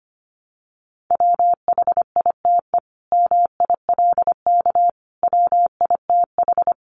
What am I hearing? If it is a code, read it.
Morse code, heard as a keyed tone, about 25 wpm, 700 Hz. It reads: WHITE MILK WITH
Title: Sentences from Top 1500 Words 25wpm